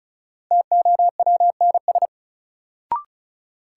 Morse code, heard as a keyed tone, about 36 wpm, 700 Hz. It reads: TOWNS
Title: Sentences from Top 1000 Words - Review Part 6 - New Word Drill - 35wpm